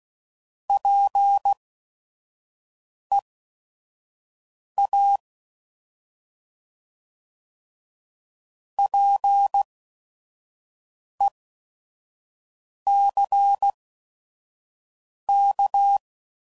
Morse code, heard as a keyed tone, about 16 wpm, spaced as for 5 wpm, 800 Hz. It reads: PEA PECK
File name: Prefixes / By First Letter / Ps